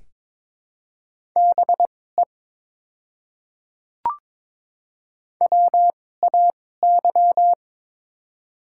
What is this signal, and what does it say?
Morse code, heard as a keyed tone, about 22 wpm, spaced as for 16 wpm, 700 Hz. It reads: BE WAY
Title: 2x Character Spacing - Top 100 Words - 22wpm